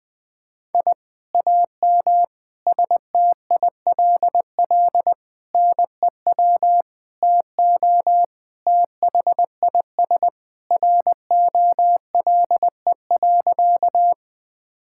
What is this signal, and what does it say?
Morse code, heard as a keyed tone, about 20 wpm, 700 Hz. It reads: I AM STILL NEW TO THIS ROLE.
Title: Sentences from Top 1000 Words - Review Part 6 - 20wpm